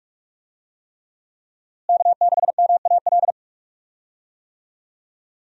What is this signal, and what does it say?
Morse code, heard as a keyed tone, about 46 wpm, 700 Hz. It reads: K6MAL
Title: USA Call Signs - 5 Characters - Repeated 3x - 45wpm